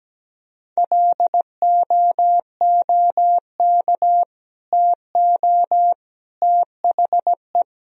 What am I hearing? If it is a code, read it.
Morse code, heard as a keyed tone, about 17 wpm, 700 Hz. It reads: LOOK TO THE
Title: Sentences from Top 1000 Words - Review Part 3 - 17wpm